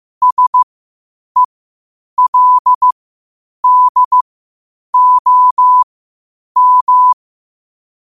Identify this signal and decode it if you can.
Morse code, heard as a keyed tone, about 15 wpm, spaced as for 8 wpm, 1000 Hz. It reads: SELDOM
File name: qotdfiles / 8wpm